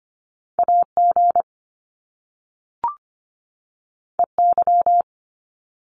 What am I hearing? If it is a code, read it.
Morse code, heard as a keyed tone, about 25 wpm, 700 Hz. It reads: AG EY
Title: Top Two Letter Combinations in Words 25wpm